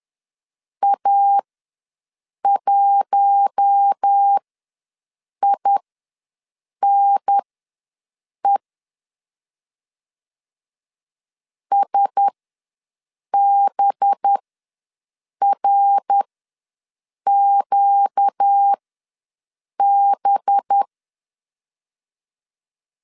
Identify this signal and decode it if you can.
Morse code, heard as a keyed tone, about 10 wpm, spaced as for 6 wpm, 800 Hz. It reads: A1INE SBRQB
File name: LEZIONI / 5 LBVC / 5 LB ws 10wpm 3616